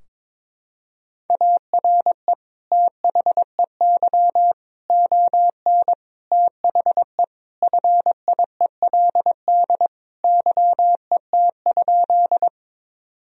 Morse code, heard as a keyed tone, about 22 wpm, 700 Hz. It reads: ARE THEY ON THE FIELD YET?
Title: Sentences from Top 1000 Words - Review Part 2 - 22wpm